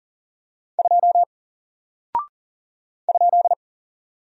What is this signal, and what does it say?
Morse code, heard as a keyed tone, about 40 wpm, 700 Hz. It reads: 2 ?
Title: Single Character (Letter:Number:Punctuation:Prosign) Alt 40wpm